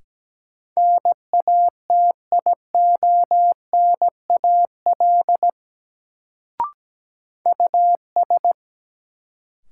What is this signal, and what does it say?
Morse code, heard as a keyed tone, about 17 wpm, 700 Hz. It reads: NATIONAL US
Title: Sets of 2 Words - New Word Drill - 17wpm